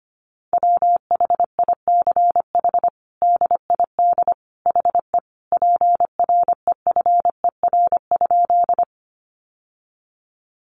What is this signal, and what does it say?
Morse code, heard as a keyed tone, about 25 wpm, 700 Hz. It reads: WHICH DID HE PREFER?
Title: Sentences from Top 1100 Words 25wpm